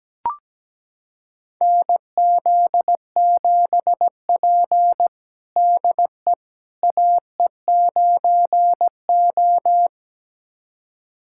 Morse code, heard as a keyed tone, about 17 wpm, 700 Hz. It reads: NZ7P DE AE9O
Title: USA Callsign DE Callsign - 4 Characters - 17wpm